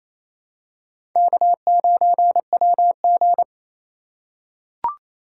Morse code, heard as a keyed tone, about 28 wpm, 700 Hz. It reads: K9WG
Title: USA Call Signs - 2x Character Spacing - 28wpm